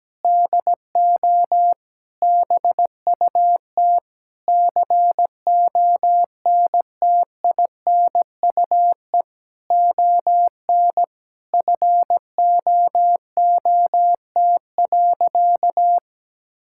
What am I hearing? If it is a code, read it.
Morse code, heard as a keyed tone, about 17 wpm, 700 Hz. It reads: DO BUT CONTINUE ON FOOT.
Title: Sentences from Top 1000 Words - Review Part 6 - 17wpm